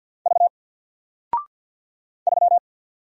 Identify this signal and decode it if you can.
Morse code, heard as a keyed tone, about 53 wpm, 700 Hz. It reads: V 3
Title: Single Letter-Number Alt 50wpm